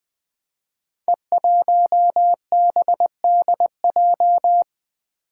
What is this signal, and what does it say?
Morse code, heard as a keyed tone, about 20 wpm, 700 Hz. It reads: E1BDJ
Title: Code Groups - 5 Letter:Numbers - 20wpm